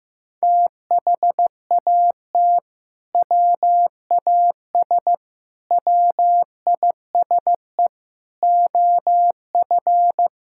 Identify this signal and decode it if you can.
Morse code, heard as a keyed tone, about 14 wpm, 700 Hz. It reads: THAT WAS WISE OF
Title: Sentences from Top 1600 Words 15wpm